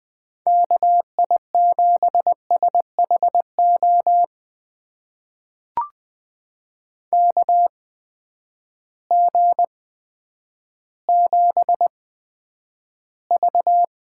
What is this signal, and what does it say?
Morse code, heard as a keyed tone, about 20 wpm, spaced as for 10 wpm, 700 Hz. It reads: KI7SHO KG7V
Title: USA Call Signs - 6 Characters - 8x Character Spacing - 20wpm